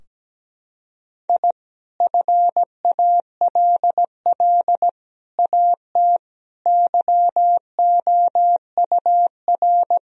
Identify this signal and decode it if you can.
Morse code, heard as a keyed tone, about 17 wpm, 700 Hz. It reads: I FALL AT YOUR
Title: Sentences from Top 1000 Words - Review Part 2 - 17wpm